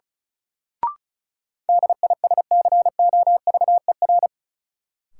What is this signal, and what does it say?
Morse code, heard as a keyed tone, about 36 wpm, 700 Hz. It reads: DISCOVER
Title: Sentences from Top 1100 Words - New Word Drill Alt - 35wpm